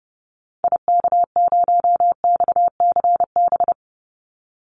Morse code, heard as a keyed tone, about 30 wpm, 700 Hz. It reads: IK0XCB
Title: Call Signs - 30wpm